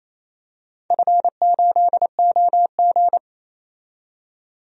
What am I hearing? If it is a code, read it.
Morse code, heard as a keyed tone, about 28 wpm, 700 Hz. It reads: F8OG
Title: DXCC-100 Call Signs - 4 Characters - 28wpm